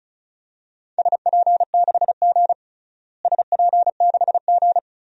Morse code, heard as a keyed tone, about 34 wpm, 700 Hz. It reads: SP6G SP6G
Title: DXCC-100 Call Signs - 4 Characters - Repeated 2x - 35wpm